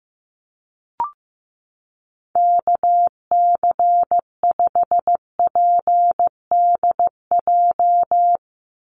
Morse code, heard as a keyed tone, about 15 wpm, 700 Hz. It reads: KC5PDJ
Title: USA Call Signs 15wpm